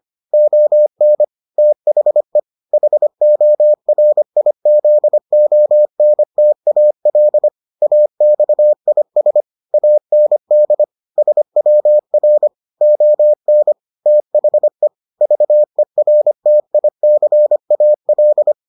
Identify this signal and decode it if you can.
Morse code, heard as a keyed tone, about 25 wpm, 600 Hz. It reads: ON THE HORIZONTAL AXIS AND SWR ON THE VERTICAL